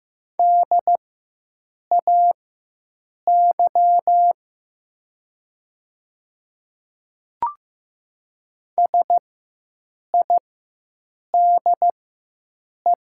Morse code, heard as a keyed tone, about 15 wpm, spaced as for 7 wpm, 700 Hz. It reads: DAY SIDE
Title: 4x Character Spacing - Top 100 Words - 15wpm